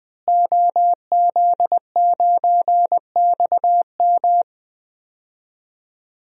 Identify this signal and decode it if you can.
Morse code, heard as a keyed tone, about 20 wpm, 700 Hz. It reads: OZ9XM
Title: DXCC-100 Call Signs - 5 Characters - Repeated 2x - 20wpm